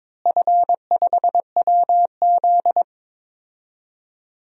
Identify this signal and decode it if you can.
Morse code, heard as a keyed tone, about 22 wpm, 700 Hz. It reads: F5WZ
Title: DXCC-100 Call Signs - 4 Characters - Repeated 3x - 22wpm